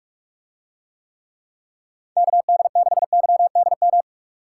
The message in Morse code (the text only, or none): KD6YDM